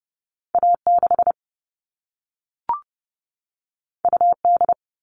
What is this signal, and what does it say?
Morse code, heard as a keyed tone, about 30 wpm, 700 Hz. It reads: A 6 U D
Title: All Cut Numbers - 2-Digit 30wpm